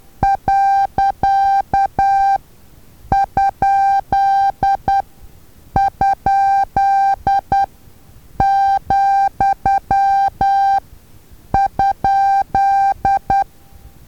Morse code, heard as a keyed tone, about 10 wpm, spaced as for 7 wpm, 790 Hz. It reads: .??,?
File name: Taviro HA6VA A